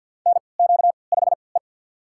Morse code, heard as a keyed tone, about 50 wpm, spaced as for 28 wpm, 700 Hz. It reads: NQ5E
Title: USA Call Signs - 4 Characters - 3x Character Spacing - 50wpm